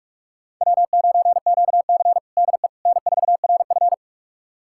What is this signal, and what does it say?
Morse code, heard as a keyed tone, about 42 wpm, 690 Hz. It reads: W9QC DE N4RF
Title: USA Callsign DE Callsign - 4 Characters - 45wpm